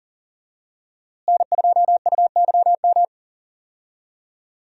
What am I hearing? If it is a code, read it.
Morse code, heard as a keyed tone, about 40 wpm, 700 Hz. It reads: N2UYM
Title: USA Call Signs - 5 Characters - Repeated 2x - 40wpm